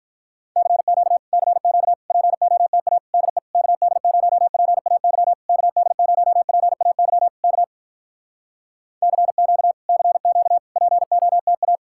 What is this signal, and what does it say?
Morse code, heard as a keyed tone, about 46 wpm, 700 Hz. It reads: CQ CQ POTA DE KD0PAY KD0PAY K CQ CQ POTA